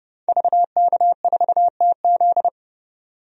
Morse code, heard as a keyed tone, about 32 wpm, 700 Hz. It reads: VK4TZ
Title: DXCC-100 Call Signs - 5 Characters - 30wpm